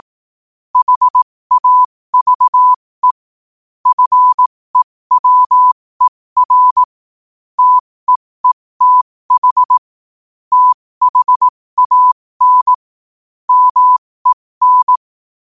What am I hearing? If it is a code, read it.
Morse code, heard as a keyed tone, about 18 wpm, spaced as for 15 wpm, 1000 Hz. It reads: HAVE FEWER TEETH THAN MEN
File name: qotdfiles / 15wpm